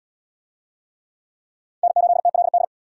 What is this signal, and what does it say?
Morse code, heard as a keyed tone, about 62 wpm, 700 Hz. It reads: N8TLD
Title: Speed Racing - USA 1x3 Call Signs 50wpm